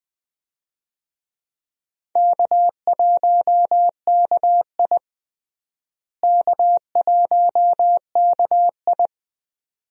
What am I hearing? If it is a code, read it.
Morse code, heard as a keyed tone, about 21 wpm, 700 Hz. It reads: K1KI K1KI
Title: USA Call Signs - 4 Characters - Repeated 2x - 20wpm